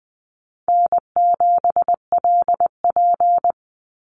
Morse code, heard as a keyed tone, about 20 wpm, 700 Hz. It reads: N7LP
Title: NA SOTA Chaser Callsigns 20wpm